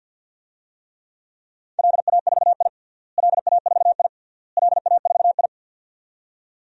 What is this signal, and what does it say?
Morse code, heard as a keyed tone, about 52 wpm, 700 Hz. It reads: LA4I LA4I LA4I
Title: DXCC-100 Call Signs - 4 Characters - Repeated 3x - 50wpm